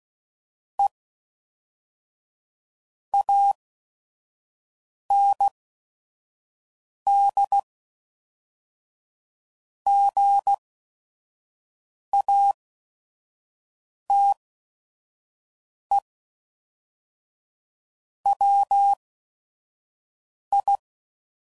Morse code, heard as a keyed tone, about 16 wpm, spaced as for 5 wpm, 800 Hz. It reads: E AND GATE WI
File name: CW Code Groups @5WPM / Technical Subjects / AND Gate